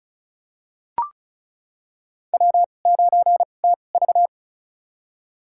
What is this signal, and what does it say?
Morse code, heard as a keyed tone, about 36 wpm, spaced as for 25 wpm, 700 Hz. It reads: W9TV
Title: USA Call Signs - 4 Characters - 2x Character Spacing - 35wpm